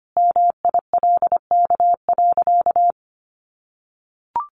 Morse code, heard as a keyed tone, about 25 wpm, 700 Hz. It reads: MILK.
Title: Sentences from Top 1500 Words 25wpm